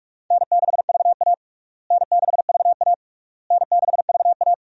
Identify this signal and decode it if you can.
Morse code, heard as a keyed tone, about 46 wpm, 700 Hz. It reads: N6VA N6VA N6VA
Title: USA Call Signs - 4 Characters - Repeated 3x - 45wpm